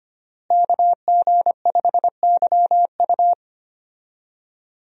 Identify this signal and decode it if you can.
Morse code, heard as a keyed tone, about 25 wpm, 700 Hz. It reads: KG5YU
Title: USA Call Signs - 5 Characters - Repeated 2x - 25wpm